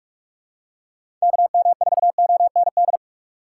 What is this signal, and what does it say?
Morse code, heard as a keyed tone, about 44 wpm, 700 Hz. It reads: KM4OND
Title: USA Call Signs - 6 Characters - 2x Character Spacing - 45wpm